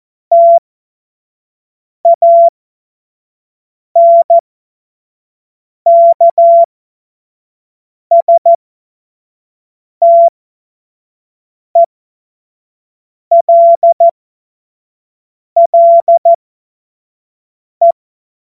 Morse code, heard as a keyed tone, about 14 wpm, spaced as for 5 wpm, 680 Hz. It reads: TANKSTELLE
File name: Morsekurs / Witze / Witz 10. August 2021, 25 F70